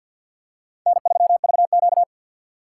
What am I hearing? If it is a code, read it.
Morse code, heard as a keyed tone, about 54 wpm, 700 Hz. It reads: N3VQ
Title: USA Call Signs - 4 Characters - 6x Character Spacing - 50wpm